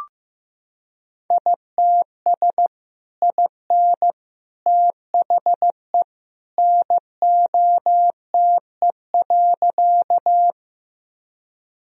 Morse code, heard as a keyed tone, about 15 wpm, 700 Hz. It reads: ITS IN THE NOTE.
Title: Sentences from Top 1000 Words - Review Part 2 - 15wpm